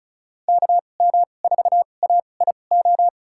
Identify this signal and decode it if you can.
Morse code, heard as a keyed tone, about 34 wpm, spaced as for 25 wpm, 700 Hz. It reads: KM4AIO
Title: USA Call Signs - 6 Characters - 2x Character Spacing - 35wpm